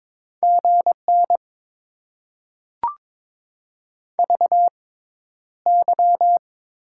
Morse code, heard as a keyed tone, about 22 wpm, spaced as for 11 wpm, 700 Hz. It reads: GN VY
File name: Top 90 QSO Elements - 6x Character Spacing - 22wpm